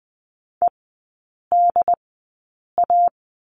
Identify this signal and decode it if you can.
Morse code, heard as a keyed tone, about 20 wpm, spaced as for 8 wpm, 690 Hz. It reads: EDA
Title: 145 Lesson 24e - Letter y - Three Characters 20wpm